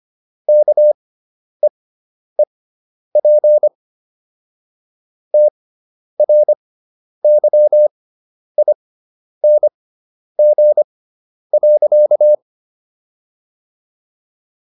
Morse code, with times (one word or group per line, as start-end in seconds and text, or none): 0.5-3.7 KEEP
5.3-12.3 TRYING.